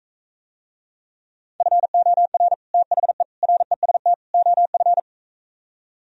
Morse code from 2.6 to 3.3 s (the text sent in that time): THE